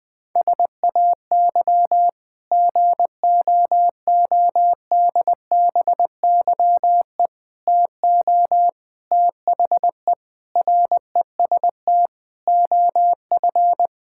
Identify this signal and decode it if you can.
Morse code, heard as a keyed tone, about 20 wpm, 700 Hz. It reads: SAY GOODBYE TO THE REST OF
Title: Sentences with 5-Word Phrases - 20wpm